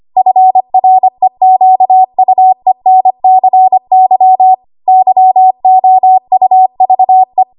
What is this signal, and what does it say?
Morse code, heard as a keyed tone, about 25 wpm, 750 Hz. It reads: FREQUENCY YOUVE
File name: Archive / 25 WPM / 25WPM